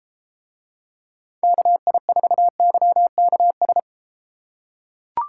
Morse code, heard as a keyed tone, about 30 wpm, 700 Hz. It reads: KI4YKS